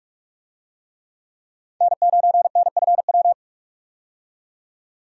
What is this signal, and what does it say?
Morse code, heard as a keyed tone, about 41 wpm, 700 Hz. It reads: N9NFW
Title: USA Call Signs - 5 Characters - 45wpm